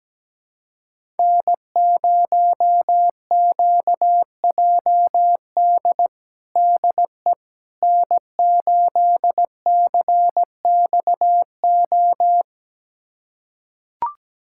Morse code, heard as a keyed tone, about 17 wpm, 700 Hz. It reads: N0QJD DE N8CXO
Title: USA Callsign DE Callsign - 5 Characters - 17wpm